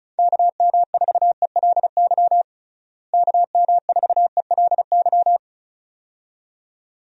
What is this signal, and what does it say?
Morse code, heard as a keyed tone, about 34 wpm, 700 Hz. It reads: KM4ELY KM4ELY
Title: USA Call Signs - 6 Characters - Repeated 2x - 35wpm